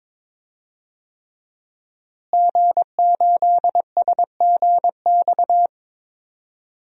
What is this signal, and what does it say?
Morse code, heard as a keyed tone, about 21 wpm, 700 Hz. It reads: G8SGX
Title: DXCC-100 Call Signs - 5 Characters - Repeated 3x - 22wpm